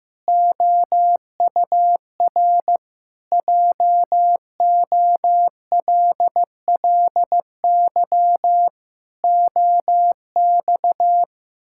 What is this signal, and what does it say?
Morse code, heard as a keyed tone, about 15 wpm, 700 Hz. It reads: OUR JOLLY OX
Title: Pangram Challenge v2 - 15wpm